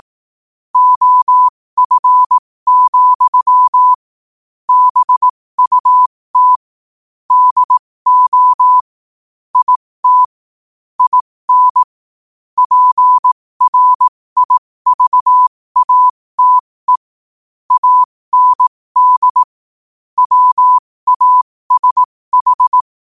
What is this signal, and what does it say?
Morse code, heard as a keyed tone, about 18 wpm, spaced as for 15 wpm, 1000 Hz. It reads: OF, BUT DO IT IN PRIVATE AND WASH
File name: qotdfiles / 15wpm